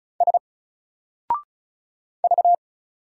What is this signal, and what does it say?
Morse code, heard as a keyed tone, about 35 wpm, 700 Hz. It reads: S V
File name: Single-Letter-Number-35wpm